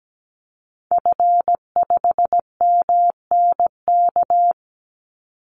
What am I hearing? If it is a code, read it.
Morse code, heard as a keyed tone, about 17 wpm, 700 Hz. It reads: F5MNK
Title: Call Signs - 17wpm